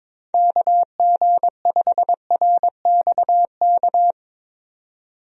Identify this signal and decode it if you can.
Morse code, heard as a keyed tone, about 22 wpm, 700 Hz. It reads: KG5RXK
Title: USA Call Signs - 6 Characters - 22wpm